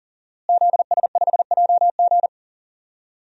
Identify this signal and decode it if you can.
Morse code, heard as a keyed tone, about 40 wpm, 700 Hz. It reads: ZS5JG
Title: DXCC-100 Call Signs - 5 Characters - Repeated 3x - 40wpm